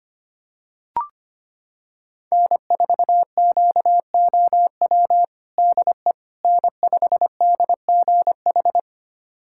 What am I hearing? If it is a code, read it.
Morse code, heard as a keyed tone, about 25 wpm, 700 Hz. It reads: N4QOW DE N5DGH
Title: USA Callsign DE Callsign - 5 Characters - 25wpm